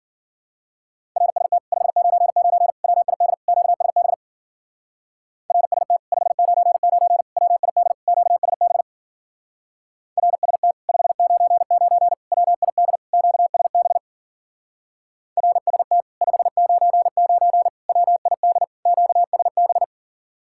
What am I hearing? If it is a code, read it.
Morse code, heard as a keyed tone, about 51 wpm, 700 Hz. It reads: RST 599 WID QSB RST 599 WID QSB RST 599 WID QSB RST 599 WID QSB